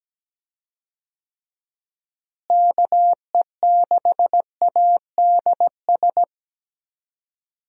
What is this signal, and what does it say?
Morse code, heard as a keyed tone, about 17 wpm, 700 Hz. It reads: KE6ADS
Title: USA Call Signs - 6 Characters - Repeated 2x - 17wpm